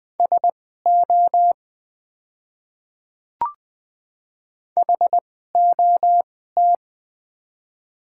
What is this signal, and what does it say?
Morse code, heard as a keyed tone, about 20 wpm, spaced as for 15 wpm, 700 Hz. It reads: SO HOT
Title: 2x Character Spacing - Top 100 Words - 20wpm